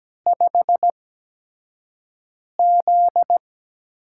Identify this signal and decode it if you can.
Morse code, heard as a keyed tone, about 17 wpm, spaced as for 5 wpm, 700 Hz. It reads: 5Z
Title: USA Call Signs - 4 Characters - 8x Character Spacing - 17wpm